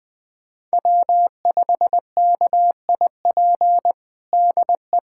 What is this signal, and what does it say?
Morse code, heard as a keyed tone, about 20 wpm, 700 Hz. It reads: W5KIP DE
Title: USA Callsign DE Callsign - 5 Characters - 20wpm